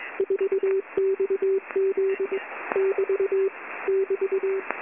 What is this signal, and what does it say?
Morse code, heard as a keyed tone, about 21 wpm, 390 Hz. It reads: 4XZ <BT> <BT>
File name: Short Wave Radio Noise 4